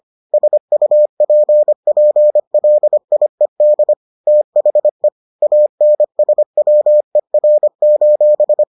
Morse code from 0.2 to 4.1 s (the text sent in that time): SUPPLIED